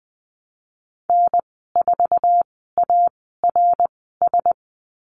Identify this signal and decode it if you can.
Morse code, heard as a keyed tone, about 20 wpm, spaced as for 15 wpm, 700 Hz. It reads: N4ARS
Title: 055 Lesson 10g - Letter r - Call Signs 20wpm